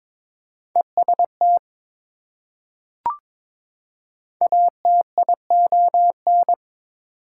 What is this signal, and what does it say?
Morse code, heard as a keyed tone, about 22 wpm, 700 Hz. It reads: EST ATION
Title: Prefixes and Suffixes 22wpm